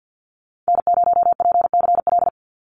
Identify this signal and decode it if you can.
Morse code, heard as a keyed tone, about 50 wpm, 700 Hz. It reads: N0PCL
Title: NA SOTA Activator Callsigns 50wpm